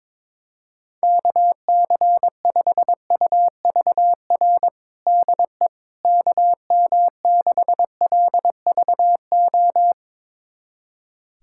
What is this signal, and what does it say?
Morse code, heard as a keyed tone, about 22 wpm, 700 Hz. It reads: KC5UVR DE KM6LVO
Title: USA Callsign DE Callsign 22wpm